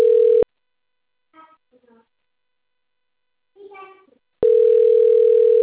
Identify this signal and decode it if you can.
Ringback tone (North America), 440 + 480 Hz, with speech.
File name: out- - - - - .13824